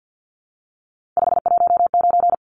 Morse code, heard as a keyed tone, about 50 wpm, 700 Hz. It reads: 519
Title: Numbers - 3-Digit 50wpm